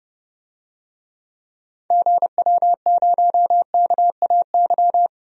Morse code, heard as a keyed tone, about 30 wpm, 700 Hz. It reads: GW0KAY
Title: DXCC-100 Call Signs - 6 Characters - 30wpm